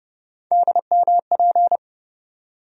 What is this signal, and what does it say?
Morse code, heard as a keyed tone, about 30 wpm, 700 Hz. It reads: DMP
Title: USA Call Sign Postfixes - 3 Letters 30wpm